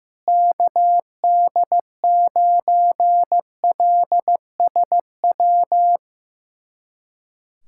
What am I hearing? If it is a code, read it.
Morse code, heard as a keyed tone, about 15 wpm, 700 Hz. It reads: KD9LSW
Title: USA Callsign DE Callsign 15wpm